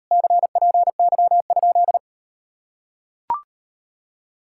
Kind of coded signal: Morse code

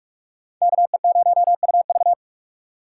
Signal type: Morse code

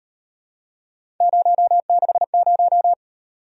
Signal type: Morse code

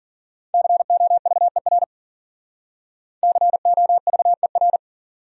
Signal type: Morse code

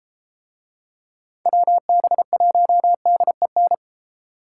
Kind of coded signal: Morse code

